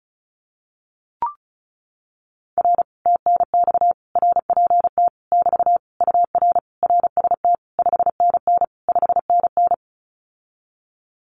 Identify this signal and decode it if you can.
Morse code, heard as a keyed tone, about 36 wpm, 700 Hz. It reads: R TNX RPT <BT> UR RST 5NN 5NN
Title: Common QSO Phrases Encore 35wpm